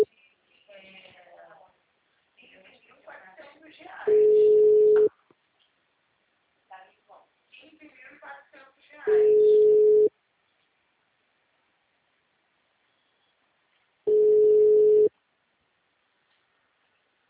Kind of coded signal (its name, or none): telephone call progress tone